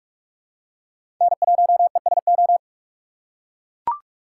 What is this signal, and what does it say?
Morse code, heard as a keyed tone, about 43 wpm, 700 Hz. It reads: N1ESO
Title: USA Call Signs - 5 Characters - 6x Character Spacing - 45wpm